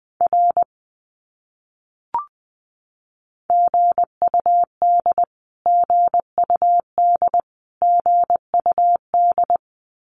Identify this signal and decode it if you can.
Morse code, heard as a keyed tone, about 20 wpm, 700 Hz. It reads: R GUD GUD GUD
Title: Abbreviations Repeated 3x - 20wpm